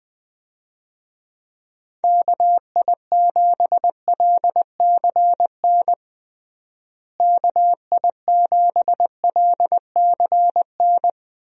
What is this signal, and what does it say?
Morse code, heard as a keyed tone, about 21 wpm, 700 Hz. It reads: KI7LCN KI7LCN
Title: USA Call Signs - 6 Characters - Repeated 2x - 20wpm